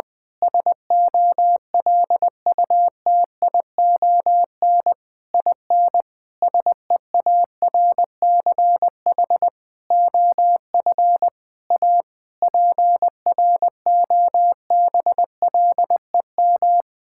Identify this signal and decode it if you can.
Morse code, heard as a keyed tone, about 20 wpm, 700 Hz. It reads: SOLUTION IN SEARCH OF A PROBLEM
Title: Sets of 8 Words - 20wpm